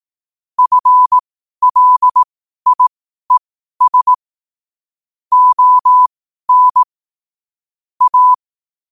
Morse code, heard as a keyed tone, about 18 wpm, spaced as for 12 wpm, 1000 Hz. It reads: FLIES ON A
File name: qotdfiles / 12wpm